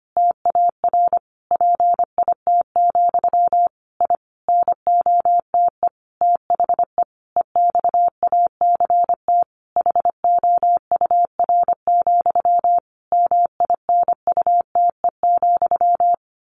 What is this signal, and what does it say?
Morse code, heard as a keyed tone, about 25 wpm, 700 Hz. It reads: TAR PIT, I NOTE THE EXACT HOUR, MINUTE,
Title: A Sound Of Thunder 25wpm